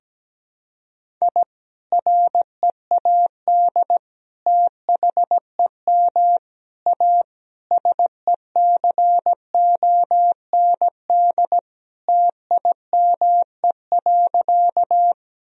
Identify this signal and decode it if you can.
Morse code, heard as a keyed tone, about 17 wpm, 700 Hz. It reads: I READ THEM A SECOND TIME.
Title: Sentences from Top 1000 Words - Review Part 1 - 17wpm